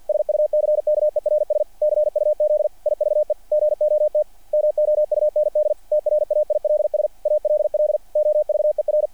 Morse code, heard as a keyed tone, about 48 wpm, 590 Hz. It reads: SUCKERS CUZ IVE GOT MOUND TRAILS ALL OVER
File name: archived / CRSnet